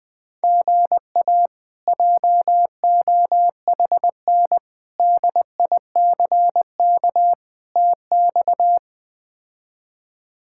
Morse code, heard as a keyed tone, about 20 wpm, 700 Hz. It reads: GA JOHN DICK TX
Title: K1USN SST QSO Practice - 20wpm